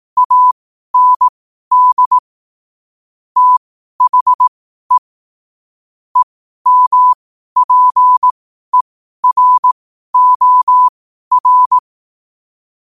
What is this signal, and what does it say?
Morse code, heard as a keyed tone, about 18 wpm, spaced as for 12 wpm, 1000 Hz. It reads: AND THE EMPEROR